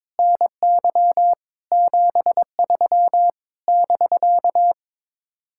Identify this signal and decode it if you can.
Morse code, heard as a keyed tone, about 22 wpm, 700 Hz. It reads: NY 73 <BK>
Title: Speed-Racing Common QSO Phrases - 22wpm